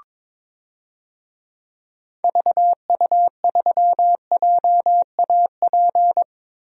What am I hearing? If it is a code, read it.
Morse code, heard as a keyed tone, about 22 wpm, 700 Hz. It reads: VU3JAP
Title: DXCC-100 Call Signs - 6 Characters - Repeated 3x - 22wpm